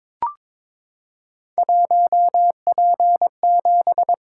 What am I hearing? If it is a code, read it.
Morse code, heard as a keyed tone, about 22 wpm, 700 Hz. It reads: 1P7
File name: practice / 22wpm / Three Letter-Number Combinations - 22wpm